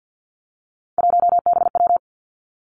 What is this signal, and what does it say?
Morse code, heard as a keyed tone, about 50 wpm, 700 Hz. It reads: 1BW
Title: Three Challenging Letter-Number Combinations 50wpm